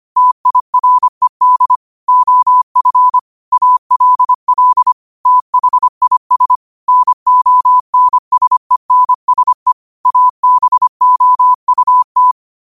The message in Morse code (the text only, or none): TIRED OF ALL THIS NONSENSE ABOUT